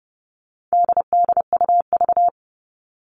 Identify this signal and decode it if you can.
Morse code, heard as a keyed tone, about 30 wpm, 700 Hz. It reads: DDUV
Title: All Cut Numbers - 4-Digit 30wpm